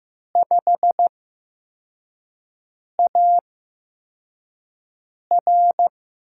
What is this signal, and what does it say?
Morse code, heard as a keyed tone, about 15 wpm, spaced as for 4 wpm, 700 Hz. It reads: 5AR
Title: USA Call Signs - 5 Characters - 8x Character Spacing - 15wpm